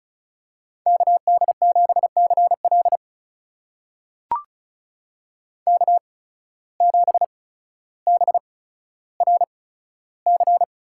Morse code, heard as a keyed tone, about 35 wpm, spaced as for 15 wpm, 690 Hz. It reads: KD7CL K7BRC
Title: USA Call Signs - 5 Characters - 8x Character Spacing - 35wpm